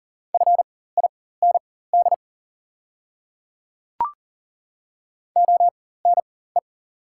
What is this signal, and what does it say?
Morse code, heard as a keyed tone, about 40 wpm, spaced as for 19 wpm, 690 Hz. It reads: FIND ONE
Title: 4x Character Spacing - Top 100 Words - 40wpm